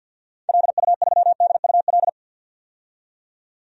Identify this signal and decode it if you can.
Morse code, heard as a keyed tone, about 53 wpm, 700 Hz. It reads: LU3DUL
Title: DXCC-100 Call Signs - 6 Characters - 50wpm